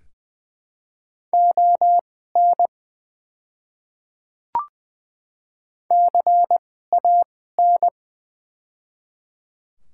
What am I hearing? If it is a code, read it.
Morse code, heard as a keyed tone, about 20 wpm, spaced as for 15 wpm, 700 Hz. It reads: ON CAN
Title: 2x Character Spacing - Top 500 Words - 20wpm